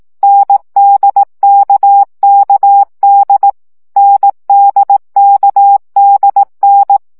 Morse code, heard as a keyed tone, about 18 wpm, 800 Hz. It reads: NDKKD NDKDN
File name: CW gyakorlas n d k 15 baud